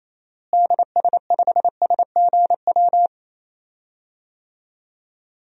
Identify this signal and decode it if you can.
Morse code, heard as a keyed tone, about 27 wpm, 700 Hz. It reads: DS5SGW
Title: DXCC-100 Call Signs - 6 Characters - Repeated 3x - 28wpm